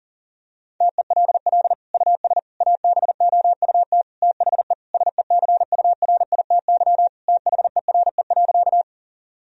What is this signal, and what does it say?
Morse code, heard as a keyed tone, about 40 wpm, 700 Hz. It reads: TELL US ABOUT THE SECURITY THERE.
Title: Sentences from Top 1000 Words - Review Part 4 - 40wpm